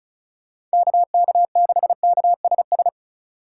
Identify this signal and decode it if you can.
Morse code, heard as a keyed tone, about 36 wpm, 700 Hz. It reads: KK6KSS
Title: USA Call Signs - 6 Characters - 3x Character Spacing - 35wpm